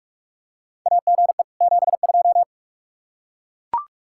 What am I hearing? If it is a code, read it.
Morse code, heard as a keyed tone, about 42 wpm, 700 Hz. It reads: AGE 72
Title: Practice QSO - Exchange 3 - More Info - 45wpm